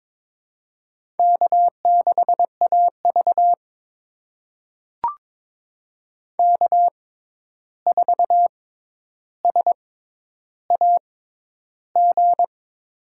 Morse code, heard as a keyed tone, about 22 wpm, spaced as for 10 wpm, 700 Hz. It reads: K6AV K4SAG
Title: USA Call Signs - 6x Character Spacing - 22wpm